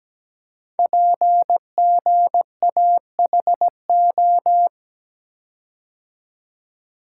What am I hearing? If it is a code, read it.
Morse code, heard as a keyed tone, about 17 wpm, 700 Hz. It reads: PGAHO
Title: Code Groups - 5 Letters - 17wpm